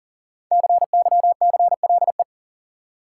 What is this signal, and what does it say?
Morse code, heard as a keyed tone, about 40 wpm, 700 Hz. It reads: CYCLE
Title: Sentences from Top 900 Words - New Word Drill - 40wpm